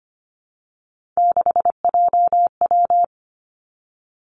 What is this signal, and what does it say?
Morse code, heard as a keyed tone, about 25 wpm, 700 Hz. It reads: 6JW
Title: Three Challenging Letter-Number Combinations 25wpm